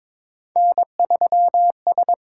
Morse code, heard as a keyed tone, about 22 wpm, 700 Hz. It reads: N3S
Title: USA Call Signs - 4 Characters - 8x Character Spacing - 22wpm